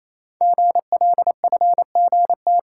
Morse code, heard as a keyed tone, about 28 wpm, 700 Hz. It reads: GLFGT
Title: Code Groups - 5 Letters - 28wpm